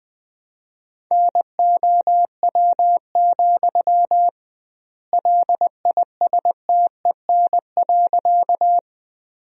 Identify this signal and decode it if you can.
Morse code, heard as a keyed tone, about 20 wpm, 700 Hz. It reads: NOW, LISTEN.
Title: 2x Word-Spacing - Sentences from Top 400 Words - 20wpm